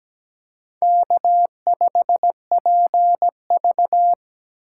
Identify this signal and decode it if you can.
Morse code, heard as a keyed tone, about 20 wpm, 700 Hz. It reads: K5PV
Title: Speed Racing - USA 1x2 Call Signs 17wpm